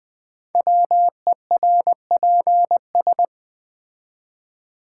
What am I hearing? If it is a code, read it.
Morse code, heard as a keyed tone, about 20 wpm, 700 Hz. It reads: WERPS
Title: Code Groups - 5 Letters - 20wpm